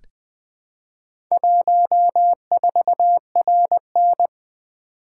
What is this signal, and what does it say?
Morse code, heard as a keyed tone, about 20 wpm, 700 Hz. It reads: 14RN
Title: Four Letter-Number Combinations - 20wpm